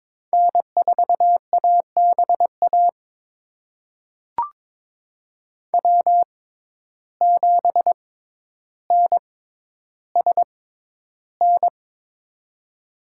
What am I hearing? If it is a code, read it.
Morse code, heard as a keyed tone, about 22 wpm, spaced as for 11 wpm, 700 Hz. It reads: N4ABA W7NSN
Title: USA Call Signs - 5 Characters - 6x Character Spacing - 22wpm